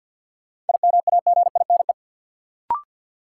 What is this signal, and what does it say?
Morse code, heard as a keyed tone, about 53 wpm, 700 Hz. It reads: IMAGINE
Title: Sentences from Top 1400 Words - New Word Drill Alt - 50wpm